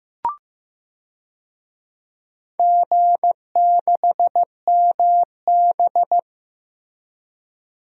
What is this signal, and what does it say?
Morse code, heard as a keyed tone, about 15 wpm, 700 Hz. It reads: G6MB
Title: DXCC-100 Call Signs - 4 Characters - Repeated 2x - 15wpm